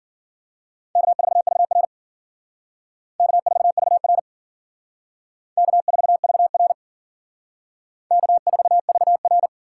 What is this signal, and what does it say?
Morse code, heard as a keyed tone, about 51 wpm, 700 Hz. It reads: K4VR K4VR K4VR K4VR